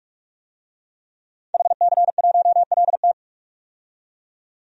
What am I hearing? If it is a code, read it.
Morse code, heard as a keyed tone, about 41 wpm, 700 Hz. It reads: HC1LT
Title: DXCC-100 Call Signs - 5 Characters - 45wpm